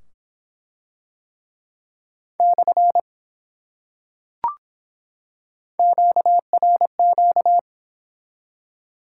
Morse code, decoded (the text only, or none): / QRQ